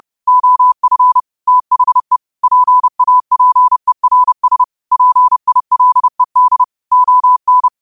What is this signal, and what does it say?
Morse code, heard as a keyed tone, about 30 wpm, 1000 Hz. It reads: OF THE PAPERS PILED ON